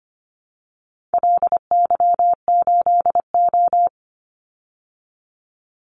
Morse code, heard as a keyed tone, about 25 wpm, 700 Hz. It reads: LY8O
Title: Call Signs - 25wpm